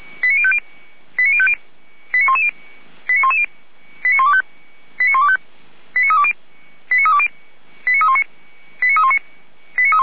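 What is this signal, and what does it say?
Five-tone selective call (ZVEI), heard as quick runs of five tones, tones of about 70 ms. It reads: 89050 89050 89110 89110 89125 89320 89320 89319 89319
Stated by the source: ZVEI five-tone selective call received on an SDR.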